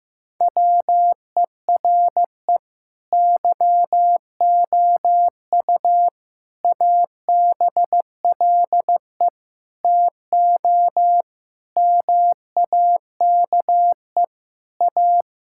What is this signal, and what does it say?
Morse code, heard as a keyed tone, about 15 wpm, 700 Hz. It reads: WERE YOU ABLE TO MAKE A
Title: Sentences from Top 1000 Words - Review Part 6 - 15wpm